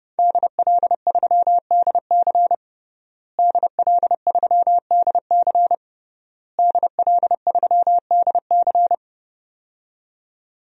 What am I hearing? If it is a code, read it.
Morse code, heard as a keyed tone, about 29 wpm, 700 Hz. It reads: DL3DC DL3DC DL3DC
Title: DXCC-100 Call Signs - 5 Characters - Repeated 3x - 30wpm